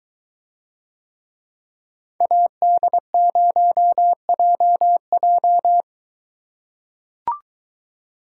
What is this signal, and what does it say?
Morse code, heard as a keyed tone, about 22 wpm, 700 Hz. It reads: AD0JJ